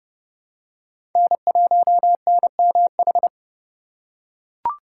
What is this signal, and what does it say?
Morse code, heard as a keyed tone, about 30 wpm, 700 Hz. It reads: N1NMH
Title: USA Call Signs - 5 Characters - 2x Character Spacing - 30wpm